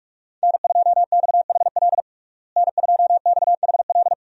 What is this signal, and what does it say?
Morse code, heard as a keyed tone, about 46 wpm, 700 Hz. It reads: N2XHL N2XHL
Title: USA Call Signs - 5 Characters - Repeated 2x - 45wpm